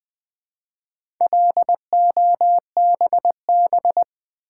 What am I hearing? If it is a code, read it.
Morse code, heard as a keyed tone, about 20 wpm, 700 Hz. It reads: LOBB
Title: Build A Word - 1 Letter At A Time - 5-Letter Words - 20wpm